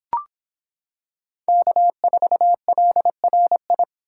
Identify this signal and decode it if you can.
Morse code, heard as a keyed tone, about 26 wpm, 700 Hz. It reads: K4LRI